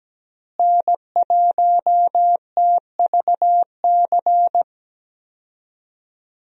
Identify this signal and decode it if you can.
Morse code, heard as a keyed tone, about 17 wpm, 700 Hz. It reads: N1TVC
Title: USA Call Signs - 5 Characters - Repeated 2x - 17wpm